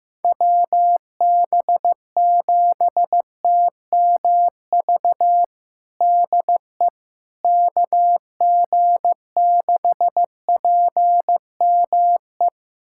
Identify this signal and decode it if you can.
Morse code, heard as a keyed tone, about 15 wpm, 700 Hz. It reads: WB7TMV DE KG6PME
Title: USA Callsign DE Callsign 15wpm